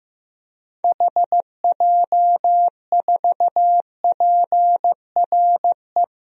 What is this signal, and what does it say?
Morse code, heard as a keyed tone, about 15 wpm, 700 Hz. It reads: HJ4PRE
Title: DXCC-100 Call Signs - 6 Characters - Repeated 2x - 15wpm